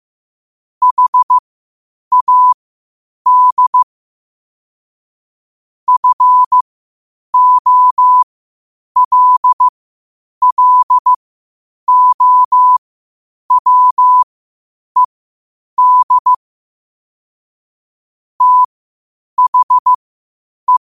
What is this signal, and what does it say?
Morse code, heard as a keyed tone, about 15 wpm, spaced as for 8 wpm, 1000 Hz. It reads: HAD FOLLOWED THE
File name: qotdfiles / 8wpm